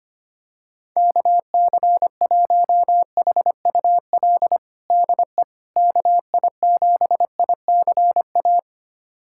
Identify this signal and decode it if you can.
Morse code, heard as a keyed tone, about 25 wpm, 700 Hz. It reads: KC1HUL DE KI7ICA
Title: USA Callsign DE Callsign - 6 Characters - 25wpm